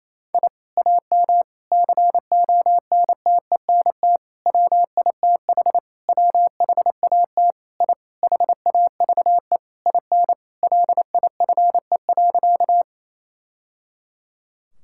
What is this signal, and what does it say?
Morse code, heard as a keyed tone, about 28 wpm, 700 Hz. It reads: I AM CONTENT WITH WHAT I HAVE IN LIFE.
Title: Sentences from Top 1000 Words - Review Part 6 - 28wpm